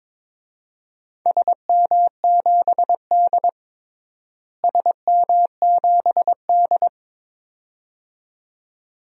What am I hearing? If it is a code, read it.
Morse code, heard as a keyed tone, about 22 wpm, 700 Hz. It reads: SM7D SM7D
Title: DXCC-100 Call Signs - 4 Characters - Repeated 2x - 22wpm